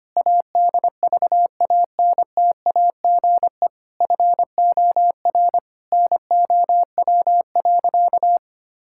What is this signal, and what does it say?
Morse code, heard as a keyed tone, about 25 wpm, 700 Hz. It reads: ADVANTAGE FOR NOW.
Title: Sentences from Top 1000 Words - Review Part 6 - 25wpm